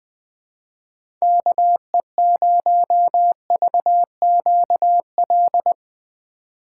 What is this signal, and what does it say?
Morse code, heard as a keyed tone, about 20 wpm, 700 Hz. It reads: KE0VQL
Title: USA Call Signs - 6 Characters - 2x Character Spacing - 20wpm